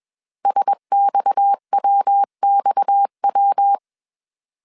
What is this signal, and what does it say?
Morse code, heard as a keyed tone, about 21 wpm, 800 Hz. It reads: SXWXW
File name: CW / DET / 1 6 1500C S A 20wpm 2048